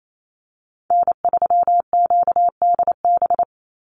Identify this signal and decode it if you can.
Morse code, heard as a keyed tone, about 28 wpm, 700 Hz. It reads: N3QDB
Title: USA Call Signs 28wpm